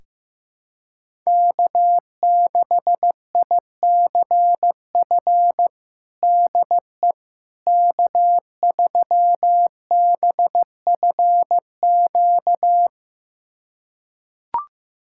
Morse code, heard as a keyed tone, about 15 wpm, 700 Hz. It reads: K6ICF DE K3BFQ
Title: USA Callsign DE Callsign - 5 Characters - 15wpm